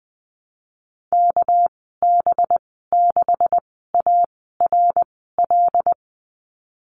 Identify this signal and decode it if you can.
Morse code, heard as a keyed tone, about 19 wpm, spaced as for 14 wpm, 700 Hz. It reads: KB6ARL
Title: 231 Lesson 37g - Letter x - Call Signs 20wpm